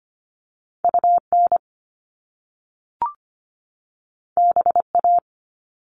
Morse code, heard as a keyed tone, about 25 wpm, 700 Hz. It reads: UN BA
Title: Top Two Letter Combinations in Words 25wpm